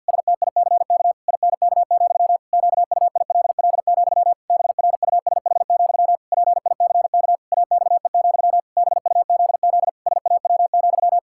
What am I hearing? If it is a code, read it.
Morse code, heard as a keyed tone, about 50 wpm, 700 Hz. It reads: STICK INK, QUILL, BRUSH, PICK AXE, BUZZ SAW,
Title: Pangram Challenge v2 - 50wpm